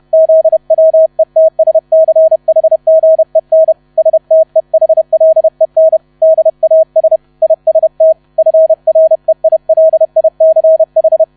Morse code, heard as a keyed tone, about 31 wpm, 640 Hz. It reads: ZWETSCHGEN STEHLEN DAS IST FREILICH
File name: 30wpm